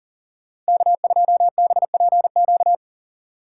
Morse code, heard as a keyed tone, about 40 wpm, 700 Hz. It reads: K2BPQ
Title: USA Call Signs - 5 Characters - 3x Character Spacing - 40wpm